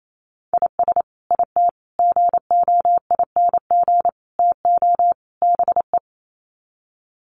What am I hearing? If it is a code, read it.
Morse code, heard as a keyed tone, about 28 wpm, 700 Hz. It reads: IS IT GOING TO BE
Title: Sets of 5 Words Encore2 28wpm